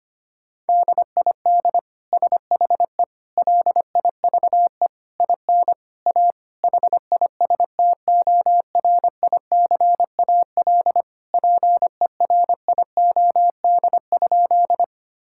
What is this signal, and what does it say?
Morse code, heard as a keyed tone, about 25 wpm, 700 Hz. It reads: DID SHE LIVE IN A HISTORICAL PERIOD?
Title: Sentences from Top 1000 Words - Review Part 5 - 25wpm